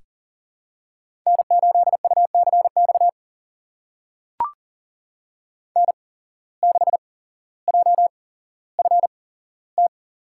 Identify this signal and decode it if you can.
Morse code, heard as a keyed tone, about 40 wpm, spaced as for 17 wpm, 700 Hz. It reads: N8UCX N6JFT
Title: USA Call Signs - 5 Characters - 8x Character Spacing - 40wpm